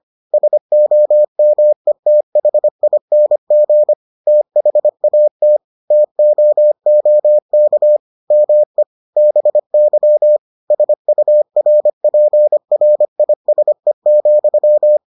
Morse code, heard as a keyed tone, about 25 wpm, 600 Hz. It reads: SOMETHING THAT TOOK ME BY SURPRISE,